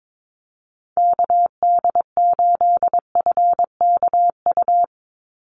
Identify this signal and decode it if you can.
Morse code, heard as a keyed tone, about 22 wpm, 700 Hz. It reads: KD8FKU
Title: USA Call Signs 22wpm